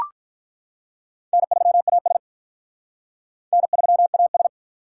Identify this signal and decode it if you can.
Morse code, heard as a keyed tone, about 42 wpm, 700 Hz. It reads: N3AS N3AS
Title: Speed Racing - USA 1x2 Call Signs 35wpm